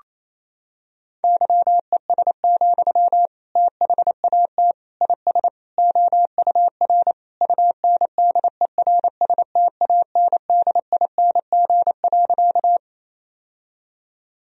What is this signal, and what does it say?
Morse code, heard as a keyed tone, about 27 wpm, 700 Hz. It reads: YES, THAT IS OUR UNDERSTANDING.
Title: Sentences from Top 1000 Words - Review Part 3 - 28wpm